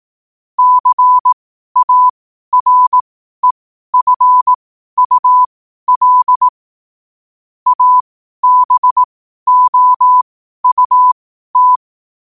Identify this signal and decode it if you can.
Morse code, heard as a keyed tone, about 18 wpm, spaced as for 12 wpm, 1000 Hz. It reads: CAREFUL ABOUT
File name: qotdfiles / 12wpm